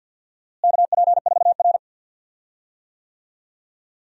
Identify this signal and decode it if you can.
Morse code, heard as a keyed tone, about 47 wpm, 700 Hz. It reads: KP4R
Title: DXCC-100 Call Signs - 4 Characters - Repeated 3x - 50wpm